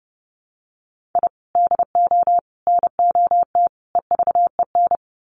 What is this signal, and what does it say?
Morse code, heard as a keyed tone, about 30 wpm, 700 Hz. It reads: I DO NOT EVEN
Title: Sets of 4 Words 30wpm